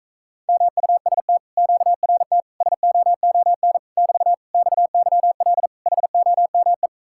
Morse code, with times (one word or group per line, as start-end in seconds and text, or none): 0.5-1.4 MUST
1.6-2.4 QRT
2.6-3.8 SOON
4.0-4.3 <BT>
4.5-5.7 XYL
5.9-6.9 HOME